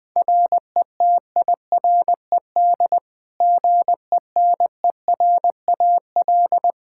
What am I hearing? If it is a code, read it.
Morse code, heard as a keyed tone, about 20 wpm, 700 Hz. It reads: RETIRED GENERAL
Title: Practice QSO - Exchange 3 - More Info - 20wpm